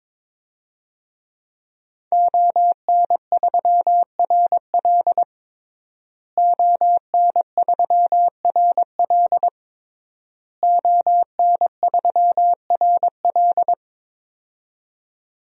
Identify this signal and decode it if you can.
Morse code, heard as a keyed tone, about 22 wpm, 700 Hz. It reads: ON3RL ON3RL ON3RL
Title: DXCC-100 Call Signs - 5 Characters - Repeated 3x - 22wpm